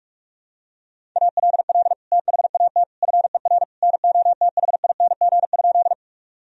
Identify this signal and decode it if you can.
Morse code, heard as a keyed tone, about 42 wpm, 690 Hz. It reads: ALL THAT FER NOTHING?
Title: Sentences from Top 500 Words and Abbreviations 45wpm